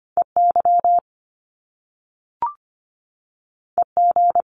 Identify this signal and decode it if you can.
Morse code, heard as a keyed tone, about 25 wpm, 700 Hz. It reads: EY EG
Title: Top Two Letter Combinations in Words 25wpm